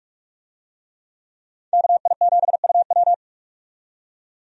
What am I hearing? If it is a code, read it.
Morse code, heard as a keyed tone, about 46 wpm, 700 Hz. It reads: KI7UW
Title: USA Call Signs - 5 Characters - 45wpm